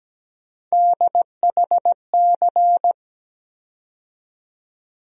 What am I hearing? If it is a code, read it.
Morse code, heard as a keyed tone, about 17 wpm, 700 Hz. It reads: DHC
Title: USA Call Sign Postfixes - 3 Letters 17wpm